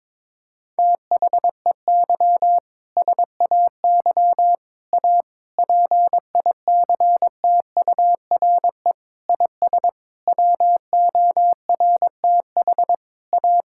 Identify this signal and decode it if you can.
Morse code, heard as a keyed tone, about 22 wpm, 700 Hz. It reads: THEY SAY A PICTURE IS WORTH A
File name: practice / 22wpm / Sentences with 7-Word Phrases - 22wpm